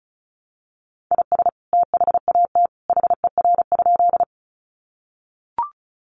Morse code, heard as a keyed tone, about 34 wpm, 700 Hz. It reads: IS THAT HER?
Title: Sentences from Top 100 Words 35wpm